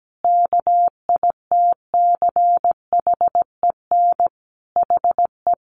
Morse code, heard as a keyed tone, about 17 wpm, 700 Hz. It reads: KITCHEN HE
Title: Sentences from Top 1400 Words 17wpm